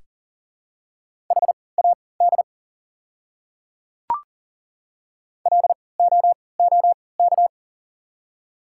Morse code, heard as a keyed tone, about 40 wpm, spaced as for 23 wpm, 700 Hz. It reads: HAD LOOK
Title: 3x Character Spacing - Top 200 Words - 40wpm